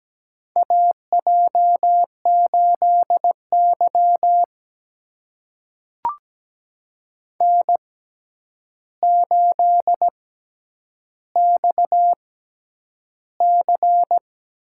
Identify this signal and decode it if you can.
Morse code, heard as a keyed tone, about 17 wpm, spaced as for 9 wpm, 700 Hz. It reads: AJ8Y N8XC
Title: USA Call Signs - 6x Character Spacing - 17wpm